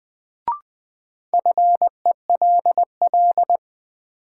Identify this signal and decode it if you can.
Morse code, heard as a keyed tone, about 20 wpm, 700 Hz. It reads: FELL
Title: Sentences from Top 1600 Words - New Word Drill Alt - 20wpm